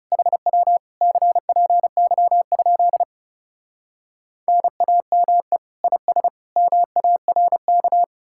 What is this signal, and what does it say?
Morse code, heard as a keyed tone, about 40 wpm, 700 Hz. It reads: HW CPY? NAME IS MARK